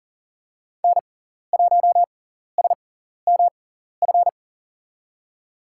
Morse code, heard as a keyed tone, about 37 wpm, spaced as for 14 wpm, 700 Hz. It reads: N1SMF
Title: USA Call Signs - 5 Characters - 6x Character Spacing - 40wpm